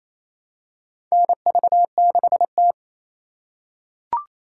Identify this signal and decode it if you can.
Morse code, heard as a keyed tone, about 28 wpm, 700 Hz. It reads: NV6T
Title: USA Call Signs - 2x Character Spacing - 28wpm